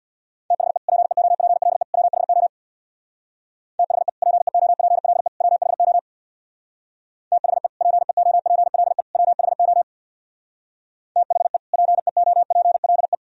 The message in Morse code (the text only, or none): THE PEOPLE WHO THE PEOPLE WHO THE PEOPLE WHO THE PEOPLE